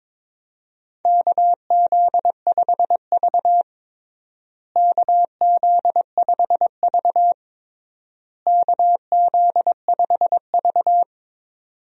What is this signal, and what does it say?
Morse code, heard as a keyed tone, about 22 wpm, 700 Hz. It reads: KZ5V KZ5V KZ5V
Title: USA Call Signs - 4 Characters - Repeated 3x - 22wpm